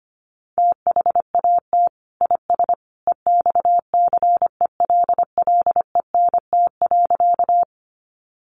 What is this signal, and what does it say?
Morse code, heard as a keyed tone, about 25 wpm, 700 Hz. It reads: THAT IS EXCELLENT.
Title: Sentences from Top 1000 Words 25wpm